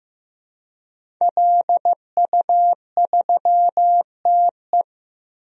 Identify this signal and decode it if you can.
Morse code, heard as a keyed tone, about 15 wpm, 700 Hz. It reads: LU3TE
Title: DXCC-100 Call Signs - 5 Characters - 15wpm